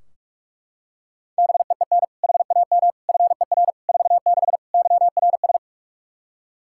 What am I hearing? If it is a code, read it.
Morse code, heard as a keyed tone, about 41 wpm, 700 Hz. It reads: BEEN HAM FER 46 YRS
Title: K1USN SST QSO Practice 45wpm